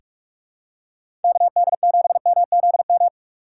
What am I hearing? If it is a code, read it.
Morse code, heard as a keyed tone, about 46 wpm, 700 Hz. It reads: KD7MZM
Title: USA Call Signs - 6 Characters - 4x Character Spacing - 45wpm